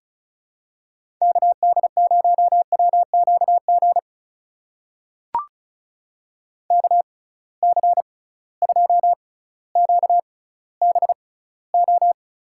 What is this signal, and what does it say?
Morse code, heard as a keyed tone, about 36 wpm, spaced as for 18 wpm, 700 Hz. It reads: KD0WQG KC2QBO
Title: USA Call Signs - 6 Characters - 6x Character Spacing - 35wpm